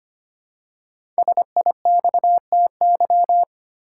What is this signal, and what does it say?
Morse code, heard as a keyed tone, about 25 wpm, 700 Hz. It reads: SIXTY
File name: 5 Letter Words - First 3 Letters Given - 25wpm